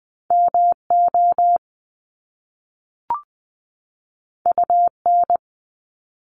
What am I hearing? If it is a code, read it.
Morse code, heard as a keyed tone, about 20 wpm, 700 Hz. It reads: MO UN
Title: Top Two Letter Combinations in Words 20wpm